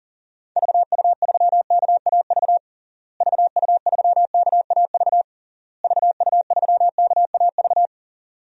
Morse code, heard as a keyed tone, about 40 wpm, 700 Hz. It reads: VU3KAV VU3KAV VU3KAV
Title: DXCC-100 Call Signs - 6 Characters - Repeated 3x - 40wpm